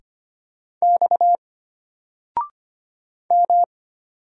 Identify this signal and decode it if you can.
Morse code, heard as a keyed tone, about 25 wpm, 700 Hz. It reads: X M
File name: Single-Letter-Number-25wpm